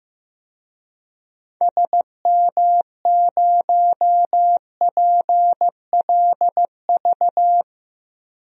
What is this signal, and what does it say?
Morse code, heard as a keyed tone, about 15 wpm, 700 Hz. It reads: SM0PLV
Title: DXCC-100 Call Signs - 6 Characters - Repeated 2x - 15wpm